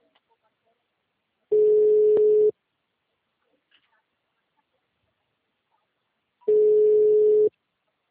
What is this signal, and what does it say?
Ringback tone (Europe, 425 Hz).
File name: out- -2004- - - .6